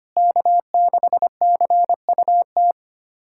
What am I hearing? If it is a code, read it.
Morse code, heard as a keyed tone, about 25 wpm, 700 Hz. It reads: K6CUT
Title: USA Callsign DE Callsign - 5 Characters - 25wpm